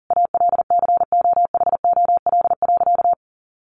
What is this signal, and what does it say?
Morse code, heard as a keyed tone, about 40 wpm, 700 Hz. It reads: ALCOHOL.
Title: Sentences from Top 1400 Words 40wpm